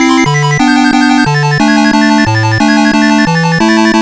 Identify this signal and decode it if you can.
There is no code.